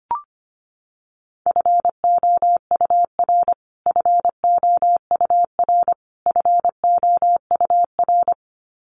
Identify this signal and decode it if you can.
Morse code, heard as a keyed tone, about 25 wpm, 700 Hz. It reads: FOUR FOUR FOUR
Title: Top 500 Words Repeated 25wpm